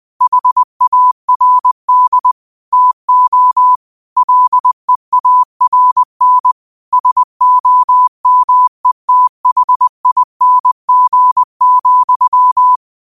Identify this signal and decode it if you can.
Morse code, heard as a keyed tone, about 20 wpm, 1000 Hz. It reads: HARD TO LEARN SOMETHING,
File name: qotdfiles / 20wpm